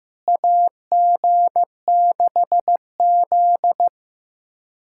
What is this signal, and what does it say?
Morse code, heard as a keyed tone, about 15 wpm, 700 Hz. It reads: AG6Z
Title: USA Callsign DE Callsign - 4 Characters - 15wpm